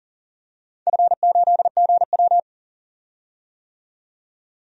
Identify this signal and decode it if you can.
Morse code, heard as a keyed tone, about 40 wpm, 700 Hz. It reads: F8GW
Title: DXCC-100 Call Signs - 4 Characters - 40wpm